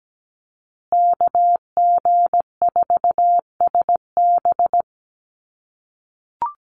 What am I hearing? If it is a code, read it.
Morse code, heard as a keyed tone, about 17 wpm, 700 Hz. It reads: KG4SB
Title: USA Call Signs Repeated 3x 17wpm